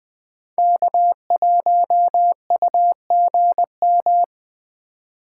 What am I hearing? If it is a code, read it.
Morse code, heard as a keyed tone, about 21 wpm, 700 Hz. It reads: K1UGM
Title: USA Call Signs - 5 Characters - 4x Character Spacing - 20wpm